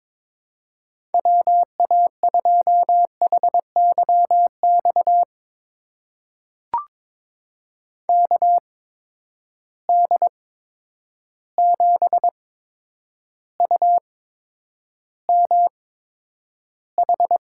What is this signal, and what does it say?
Morse code, heard as a keyed tone, about 22 wpm, spaced as for 9 wpm, 700 Hz. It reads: WA2HYX KD7UMH
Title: USA Call Signs - 6 Characters - 8x Character Spacing - 22wpm